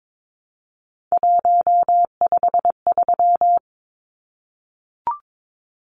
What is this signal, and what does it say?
Morse code, heard as a keyed tone, about 22 wpm, 700 Hz. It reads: 153
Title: Numbers - 3-Digit 22wpm